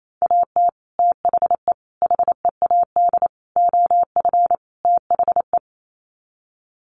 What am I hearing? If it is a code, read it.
Morse code, heard as a keyed tone, about 28 wpm, 700 Hz. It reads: AT THE HEAD OF THE
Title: Sets of 5 Words 28wpm